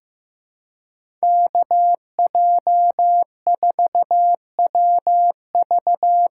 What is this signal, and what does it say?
Morse code, heard as a keyed tone, about 15 wpm, 700 Hz. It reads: KJ4WV
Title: USA Call Signs - 6 Characters - 6x Character Spacing - 15wpm